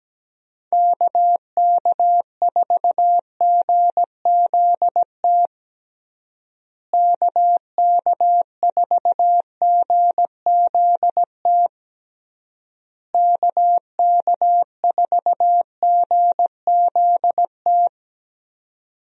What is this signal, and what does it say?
Morse code, heard as a keyed tone, about 17 wpm, 700 Hz. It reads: KK4GZT KK4GZT KK4GZT
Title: USA Call Signs - 6 Characters - Repeated 3x - 17wpm